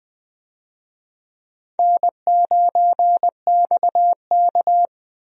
Morse code, heard as a keyed tone, about 20 wpm, 700 Hz. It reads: N9XK
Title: USA Call Signs - 4 Characters - Repeated 3x - 20wpm